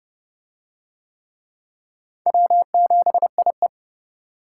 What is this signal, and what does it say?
Morse code, heard as a keyed tone, about 29 wpm, 700 Hz. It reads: W7IE